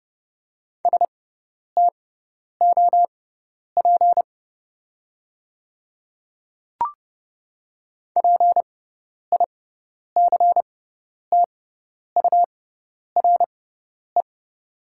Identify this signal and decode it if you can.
Morse code, heard as a keyed tone, about 30 wpm, spaced as for 10 wpm, 700 Hz. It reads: STOP PICTURE
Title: 6x Character Spacing - Top 300 Words - 30wpm